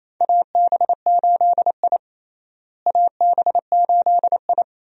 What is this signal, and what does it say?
Morse code, heard as a keyed tone, about 28 wpm, 700 Hz. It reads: AB8I AB8I
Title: USA Call Signs - 4 Characters - Repeated 2x - 28wpm